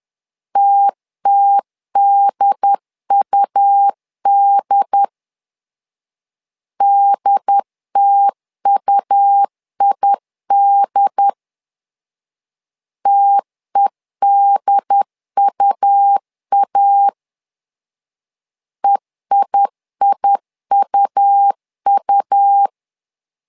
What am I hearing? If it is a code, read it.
Morse code, heard as a keyed tone, about 10 wpm, 800 Hz. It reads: TTDUD DTUID TEDUA EIIUU
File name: LEZIONI / 2 UHDN / 2 UHD 10wpm 1826